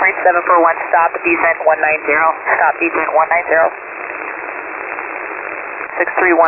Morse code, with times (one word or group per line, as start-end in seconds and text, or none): none